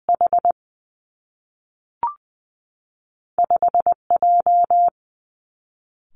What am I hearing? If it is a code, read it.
Morse code, heard as a keyed tone, about 20 wpm, 700 Hz. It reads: H 5 J
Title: Two Challenging Letter-Number Combinations 20wpm